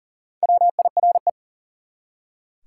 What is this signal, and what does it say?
Morse code, heard as a keyed tone, about 40 wpm, 700 Hz. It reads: WIRE
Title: Top 180 QSO Elements 40wpm